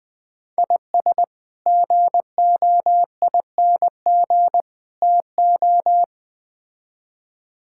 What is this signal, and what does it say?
Morse code, heard as a keyed tone, about 21 wpm, 700 Hz. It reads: IS GOING TO
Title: Build A Phrase - 4-Word Phrases - 20wpm